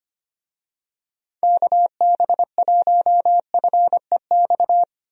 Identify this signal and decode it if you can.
Morse code, heard as a keyed tone, about 25 wpm, 700 Hz. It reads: KB1FEX
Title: USA Call Signs - 6 Characters - Repeated 2x - 25wpm